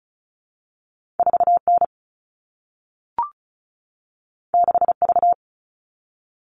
Morse code, heard as a keyed tone, about 36 wpm, 700 Hz. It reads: 4N 6V
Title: All Cut Numbers - 2-Digit 35wpm